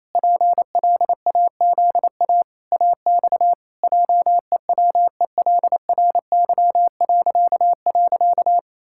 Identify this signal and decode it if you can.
Morse code, heard as a keyed tone, about 27 wpm, 700 Hz. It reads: PLAZA AX JEWELRY..
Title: Pangram Challenge v2 - 28wpm